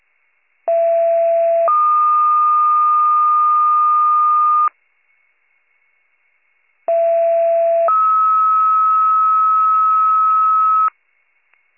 Two-tone page, heard as a one-second tone, then a three-second tone, about 670 then 1220 Hz, about 670 then 1290 Hz. It is heard from 0.7 to 10.9 s.